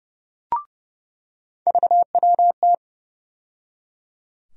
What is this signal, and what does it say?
Morse code, heard as a keyed tone, about 30 wpm, 700 Hz. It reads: VWT
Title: USA Call Sign Postfixes - 3 Letters 30wpm